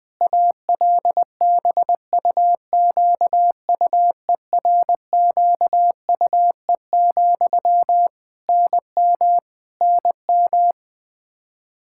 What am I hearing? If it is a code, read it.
Morse code, heard as a keyed tone, about 20 wpm, 700 Hz. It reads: ALBUQUERQUE, NM NM
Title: Practice QSO - Exchange 1 - Beginning - 20wpm